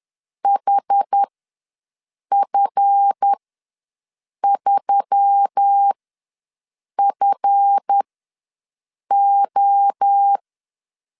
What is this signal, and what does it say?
Morse code, heard as a keyed tone, about 10 wpm, spaced as for 6 wpm, 800 Hz. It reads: HF3FO
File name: LEZIONI / 7 GF9Z / 7 GF ws 10wpm 3918